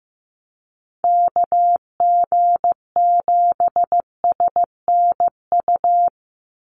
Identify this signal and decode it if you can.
Morse code, heard as a keyed tone, about 15 wpm, 700 Hz. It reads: KG7SNU
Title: USA Call Signs 15wpm